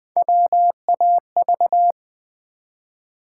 Morse code, heard as a keyed tone, about 20 wpm, 700 Hz. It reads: WAV
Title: Build A Word - 1 Letter At A Time - 3 to 5-Letter Words - 20wpm